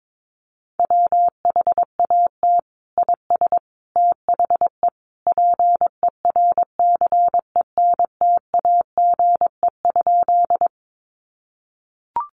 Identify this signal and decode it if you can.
Morse code, heard as a keyed tone, about 22 wpm, 700 Hz. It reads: WHAT IS THE PERCENTAGE?
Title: Sentences from Top 1300 Words 22wpm